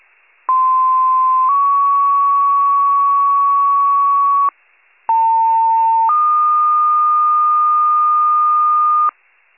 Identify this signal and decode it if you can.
Two-tone page, heard as a one-second tone, then a three-second tone, about 1040 then 1120 Hz, about 900 then 1220 Hz. It is heard from 0.5 to 9.1 s.